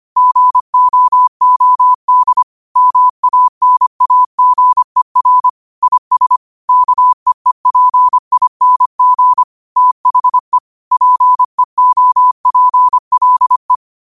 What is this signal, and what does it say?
Morse code, heard as a keyed tone, about 25 wpm, 1000 Hz. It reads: GOOD MANAGER IS KEEPING THE PEOPLE
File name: qotdfiles / 25wpm